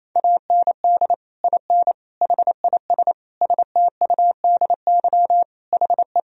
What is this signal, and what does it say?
Morse code, heard as a keyed tone, about 28 wpm, 700 Hz. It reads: AND IN HIS STUDY HE
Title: Sentences from Top 1000 Words - Review Part 1 - 28wpm